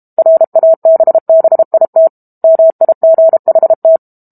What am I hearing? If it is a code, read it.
Morse code, heard as a keyed tone, about 32 wpm, 650 Hz. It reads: RABBIT MIGHT
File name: Book Two / 06-Chapter 6 - 32 WPM